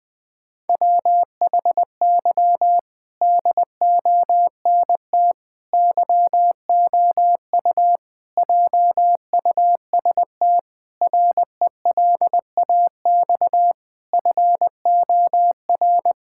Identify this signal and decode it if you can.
Morse code, heard as a keyed tone, about 20 wpm, 700 Hz. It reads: WHY DONT YOU JUST RELAX FOR
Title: Sentences with 5-Word Phrases - 20wpm